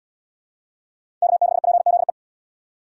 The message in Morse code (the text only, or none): K6PZE